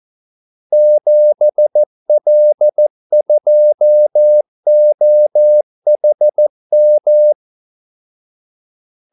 Morse code, heard as a keyed tone, about 14 wpm, 600 Hz. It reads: 7L2OHM
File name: Callsigns-14wpm